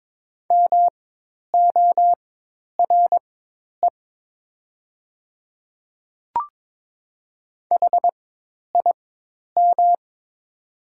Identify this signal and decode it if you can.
Morse code, heard as a keyed tone, about 22 wpm, spaced as for 10 wpm, 700 Hz. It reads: MORE HIM
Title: 4x Character Spacing - Top 100 Words - 22wpm